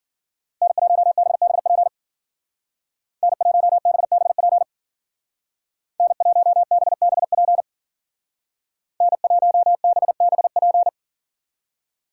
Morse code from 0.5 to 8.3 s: N1BBP N1BBP N1BBP

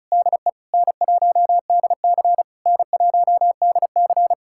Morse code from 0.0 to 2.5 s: DE N1DC

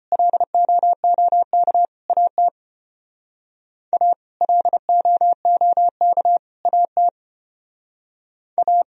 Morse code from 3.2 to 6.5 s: A LOOK

